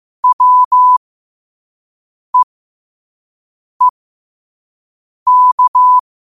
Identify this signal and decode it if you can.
Morse code, heard as a keyed tone, about 15 wpm, spaced as for 5 wpm, 1000 Hz. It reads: WEEK
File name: qotdfiles / 5wpm